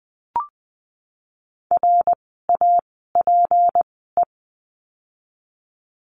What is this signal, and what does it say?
Morse code, heard as a keyed tone, about 20 wpm, spaced as for 15 wpm, 700 Hz. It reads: RAPE
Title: 153 Lesson 25f - Letter p - Words 20wpm